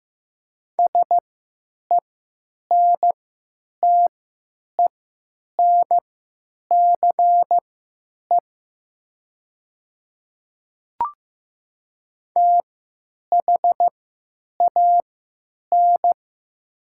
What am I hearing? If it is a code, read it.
Morse code, heard as a keyed tone, about 15 wpm, spaced as for 9 wpm, 700 Hz. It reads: SENTENCE THAN
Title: 3x Character Spacing - Top 300 Words - 15wpm